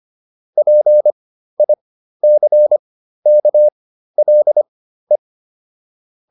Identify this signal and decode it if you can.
Morse code, heard as a keyed tone, about 25 wpm, spaced as for 13 wpm, 600 Hz. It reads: PICKLE